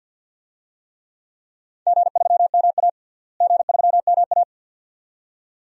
Morse code, heard as a keyed tone, about 52 wpm, 700 Hz. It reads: G3MA G3MA
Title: DXCC-100 Call Signs - 4 Characters - Repeated 2x - 50wpm